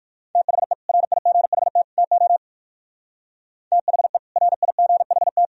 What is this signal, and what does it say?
Morse code, heard as a keyed tone, about 57 wpm, 700 Hz. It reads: THE RIGHT TO THE RIGHT